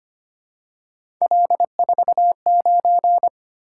Morse code, heard as a keyed tone, about 25 wpm, 700 Hz. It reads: L49
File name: Three Letter-Number Combinations - 25wpm